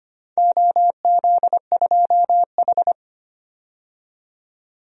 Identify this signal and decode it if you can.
Morse code, heard as a keyed tone, about 25 wpm, 700 Hz. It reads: OZ2H
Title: DXCC-100 Call Signs - 4 Characters - 25wpm